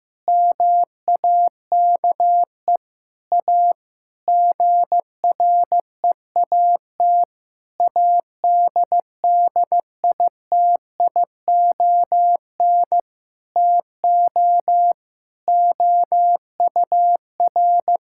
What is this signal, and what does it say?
Morse code, heard as a keyed tone, about 15 wpm, 700 Hz. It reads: MAKE A GREAT ADDITION TO OUR
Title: Sentences from Top 1000 Words - Review Part 6 - 15wpm